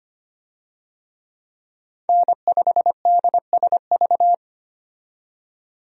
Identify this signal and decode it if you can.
Morse code, heard as a keyed tone, about 25 wpm, 690 Hz. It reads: N5DSV
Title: USA Call Signs - 5 Characters - 25wpm